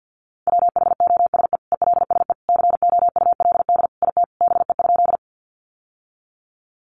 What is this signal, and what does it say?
Morse code, heard as a keyed tone, about 50 wpm, 700 Hz. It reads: WHOSE ELSE COULD IT BE?
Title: Sentences from Top 1200 Words 50wpm